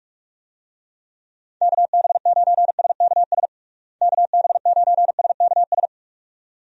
Morse code, heard as a keyed tone, about 42 wpm, 700 Hz. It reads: KB9SKS KB9SKS
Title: USA Call Signs - 6 Characters - Repeated 2x - 45wpm